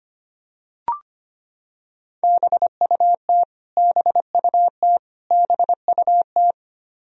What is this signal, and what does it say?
Morse code, heard as a keyed tone, about 25 wpm, 700 Hz. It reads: BUT BUT BUT
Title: Top 100 Words Repeated 25wpm